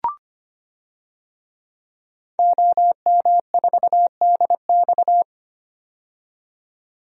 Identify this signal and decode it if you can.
Morse code, heard as a keyed tone, about 25 wpm, 700 Hz. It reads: OM4DX
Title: DXCC-100 Call Signs - 5 Characters - 25wpm